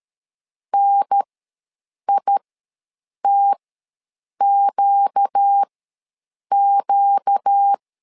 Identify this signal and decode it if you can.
Morse code, heard as a keyed tone, about 13 wpm, spaced as for 7 wpm, 800 Hz. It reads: NITQQ
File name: LEZIONI / 4 Q ws 12wpm 1912